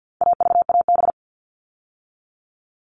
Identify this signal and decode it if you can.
Morse code, heard as a keyed tone, about 50 wpm, 700 Hz. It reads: AVAB
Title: All Cut Numbers - 4-Digit 50wpm